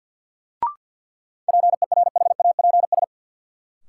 Morse code, heard as a keyed tone, about 50 wpm, 700 Hz. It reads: PERHAPS
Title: Sentences from Top 500 Words - New Word Drill - 50wpm